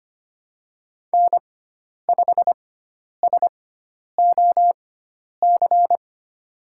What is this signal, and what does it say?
Morse code, heard as a keyed tone, about 25 wpm, spaced as for 10 wpm, 700 Hz. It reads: N5SOC
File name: Lesson 16g - Letter c - Call Signs 25wpm